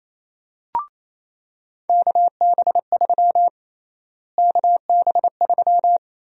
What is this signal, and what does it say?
Morse code, heard as a keyed tone, about 27 wpm, 700 Hz. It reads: KB3 KB3
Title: USA Call Sign Prefixes - Repeated 2x - 28wpm